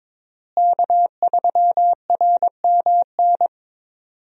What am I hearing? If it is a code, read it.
Morse code, heard as a keyed tone, about 22 wpm, 700 Hz. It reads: K3RMN
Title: USA Call Signs - 5 Characters - Repeated 3x - 22wpm